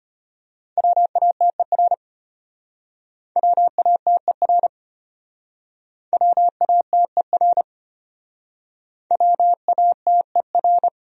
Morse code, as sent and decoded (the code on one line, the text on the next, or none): .-- .- - . .-. / .-- .- - . .-. / .-- .- - . .-. / .-- .- - . .-.
WATER WATER WATER WATER